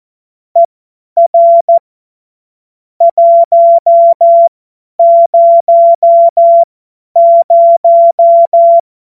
Morse code, heard as a keyed tone, about 14 wpm, spaced as for 10 wpm, 680 Hz. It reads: ER 100
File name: Morsekurs / Witze / Witz 24.Oktober 2023 50 F70